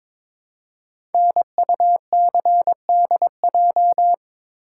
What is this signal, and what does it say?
Morse code, heard as a keyed tone, about 22 wpm, 690 Hz. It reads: NUCDJ
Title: Code Groups - 5 Letters - 22wpm